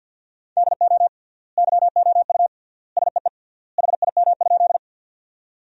Morse code, heard as a keyed tone, about 52 wpm, 700 Hz. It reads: DO YOU SEE HIM?
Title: 3x Word-Spacing - Sentences from Top 100 Words - 50wpm